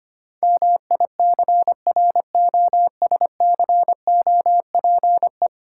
Morse code, heard as a keyed tone, about 26 wpm, 690 Hz. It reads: MICROSCOPE
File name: practice / 25wpm / Difficult Words - Speed-Racing with Hints - 25wpm